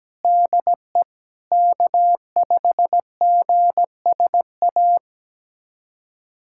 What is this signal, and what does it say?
Morse code, heard as a keyed tone, about 17 wpm, 700 Hz. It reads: DE K5GSA
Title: USA Callsign DE Callsign 17wpm